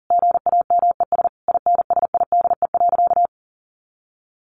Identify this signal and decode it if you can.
Morse code, heard as a keyed tone, about 40 wpm, 700 Hz. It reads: GAMES INSIDE.
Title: Easy Phrases 40wpm